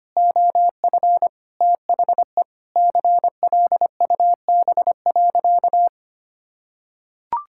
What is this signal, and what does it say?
Morse code, heard as a keyed tone, about 25 wpm, 700 Hz. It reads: OF THE CLUB.
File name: practice / 25wpm / Sentences with 3-Word Phrases - 25wpm